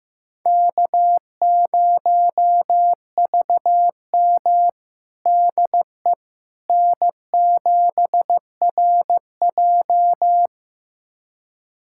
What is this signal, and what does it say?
Morse code, heard as a keyed tone, about 15 wpm, 700 Hz. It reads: K0VM DE N7RJ
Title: USA Callsign DE Callsign - 4 Characters - 15wpm